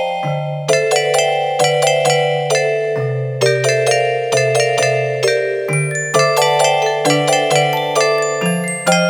There is no code.